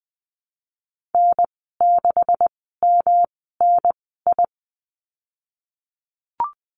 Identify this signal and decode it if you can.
Morse code, heard as a keyed tone, about 19 wpm, spaced as for 14 wpm, 700 Hz. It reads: N6MNI
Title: 128 Lesson 21g - Number 6 - Call Signs 20wpm